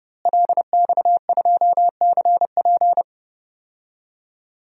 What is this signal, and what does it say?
Morse code, heard as a keyed tone, about 31 wpm, 700 Hz. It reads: LX2CP
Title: DXCC-100 Call Signs - 5 Characters - Repeated 2x - 30wpm